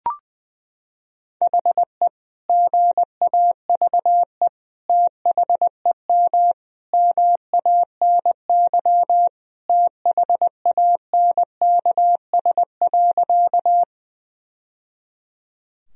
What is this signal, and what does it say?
Morse code, heard as a keyed tone, about 21 wpm, 700 Hz. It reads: HE GAVE THEM MANY THANKS.
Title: Sentences from Top 1000 Words - Review Part 3 - 20wpm